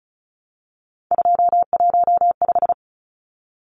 Morse code, heard as a keyed tone, about 34 wpm, 700 Hz. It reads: 215
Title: Numbers - 3-Digit 35wpm